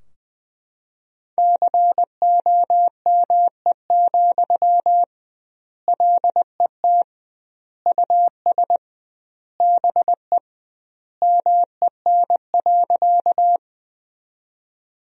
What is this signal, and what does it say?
Morse code, heard as a keyed tone, about 20 wpm, 700 Hz. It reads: COME, LET US BE MEN.
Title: 2x Word-Spacing - Sentences from Top 300 Words - 20wpm